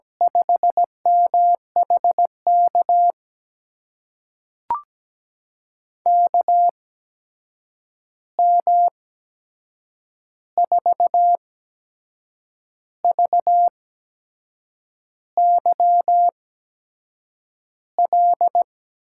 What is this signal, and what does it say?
Morse code, heard as a keyed tone, about 17 wpm, spaced as for 6 wpm, 700 Hz. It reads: 5MHK KM4VYL
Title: USA Call Signs - 6 Characters - 8x Character Spacing - 17wpm